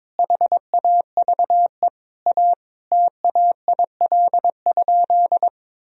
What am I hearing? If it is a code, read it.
Morse code, heard as a keyed tone, about 21 wpm, 700 Hz. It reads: HAVE A TAIL?
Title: Sentences from Top 1000 Words - Review Part 3 - 22wpm